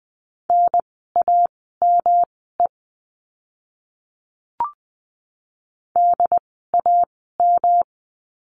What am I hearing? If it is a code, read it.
Morse code, heard as a keyed tone, about 20 wpm, spaced as for 15 wpm, 700 Hz. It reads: NAME DAM
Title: 108 Lesson 18f - Letter m - Words 20wpm